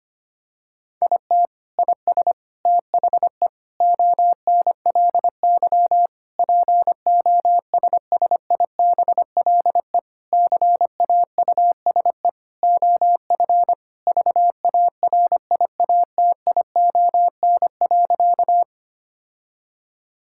Morse code, heard as a keyed tone, about 25 wpm, 700 Hz. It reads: IT IS THE ONLY POSSIBLE CAUSE OF VARIATION.
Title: Sentences from Top 1600 Words 25wpm